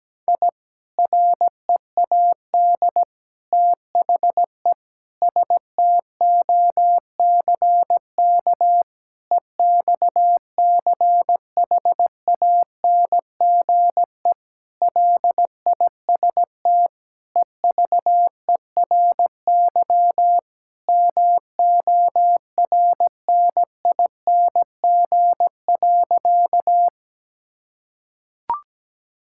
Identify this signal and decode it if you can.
Morse code, heard as a keyed tone, about 17 wpm, 700 Hz. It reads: I READ THE STOCK EXCHANGE LIST EVERY MORNING.
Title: Sentences from Top 1000 Words - Review Part 6 - 17wpm